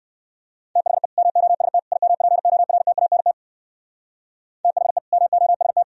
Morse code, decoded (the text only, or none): THE MOST IMPORTANT THE MOST